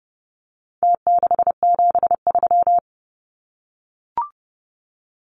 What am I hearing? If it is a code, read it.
Morse code, heard as a keyed tone, about 30 wpm, 700 Hz. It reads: T673
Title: Numbers - Contest Style - 4-Digit 30wpm